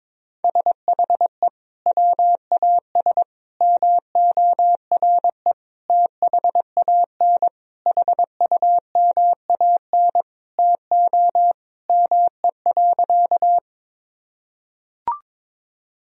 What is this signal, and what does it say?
Morse code, heard as a keyed tone, about 22 wpm, 700 Hz. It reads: SHE WAS MORE THAN HUMAN TO ME.
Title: Sentences from Top 1000 Words - Review Part 3 - 22wpm